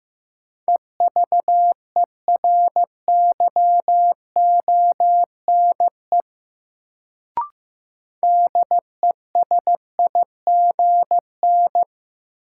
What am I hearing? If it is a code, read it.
Morse code, heard as a keyed tone, about 15 wpm, 700 Hz. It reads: EVERYONE DESIGN
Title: Sentences from Top 800 Words - New Word Drill - 15wpm